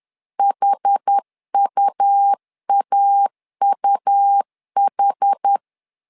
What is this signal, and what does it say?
Morse code, heard as a keyed tone, about 10 wpm, 800 Hz. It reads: HUAUH
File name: LEZIONI / 2 UHDN / 2 UH 10wpm 1714